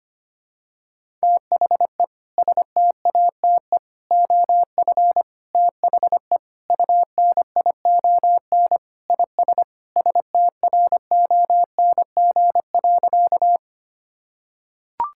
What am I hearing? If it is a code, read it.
Morse code, heard as a keyed tone, about 25 wpm, 700 Hz. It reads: THE STATE OF THE UNION IS STRONG.
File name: practice / 25wpm / Sentences with 3-Word Phrases - 25wpm